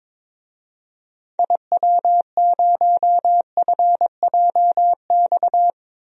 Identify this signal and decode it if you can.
Morse code, heard as a keyed tone, about 22 wpm, 700 Hz. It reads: IW0FJX
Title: DXCC-100 Call Signs - 6 Characters - 22wpm